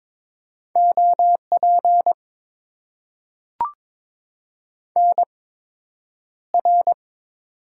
Morse code, heard as a keyed tone, about 21 wpm, spaced as for 9 wpm, 700 Hz. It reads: OP NR
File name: Top 90 QSO Elements - 8x Character Spacing - 22wpm